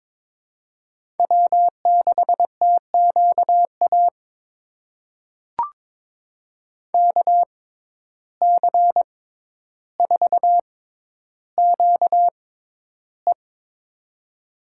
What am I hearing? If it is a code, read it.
Morse code, heard as a keyed tone, about 22 wpm, spaced as for 11 wpm, 700 Hz. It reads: W6TQA KC4QE
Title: USA Call Signs - 5 Characters - 6x Character Spacing - 22wpm